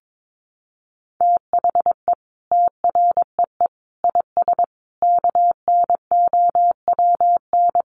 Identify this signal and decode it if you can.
Morse code, heard as a keyed tone, about 22 wpm, 700 Hz. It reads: THE TREE IS KNOWN
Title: Sentences from Top 1300 Words 22wpm